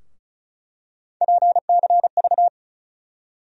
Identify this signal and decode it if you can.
Morse code, heard as a keyed tone, about 36 wpm, 700 Hz. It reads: PCV
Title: USA Call Sign Postfixes - 3 Letters 35wpm